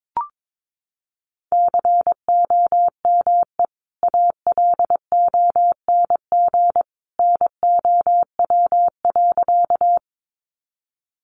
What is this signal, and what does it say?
Morse code, heard as a keyed tone, about 22 wpm, 700 Hz. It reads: COME ALONG NOW.
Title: Sentences from Top 700 Words 22wpm